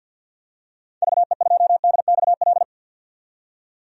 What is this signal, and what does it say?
Morse code, heard as a keyed tone, about 51 wpm, 700 Hz. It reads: VE2DXL
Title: DXCC-100 Call Signs - 6 Characters - 50wpm